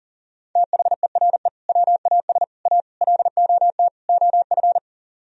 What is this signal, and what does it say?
Morse code, heard as a keyed tone, about 50 wpm, 700 Hz. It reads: THERE WAS A LOT OF